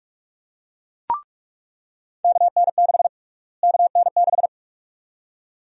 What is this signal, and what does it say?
Morse code, heard as a keyed tone, about 44 wpm, 700 Hz. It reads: KN6 KN6
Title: USA Call Sign Prefixes - Repeated 2x - 45wpm